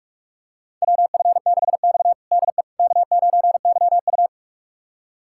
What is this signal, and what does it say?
Morse code, heard as a keyed tone, about 42 wpm, 700 Hz. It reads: WF6X DE K9YU
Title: USA Callsign DE Callsign - 4 Characters - 45wpm